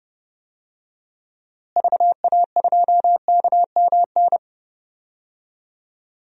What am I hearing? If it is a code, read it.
Morse code, heard as a keyed tone, about 29 wpm, 700 Hz. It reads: VA2KMN
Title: DXCC-100 Call Signs - 6 Characters - Repeated 3x - 30wpm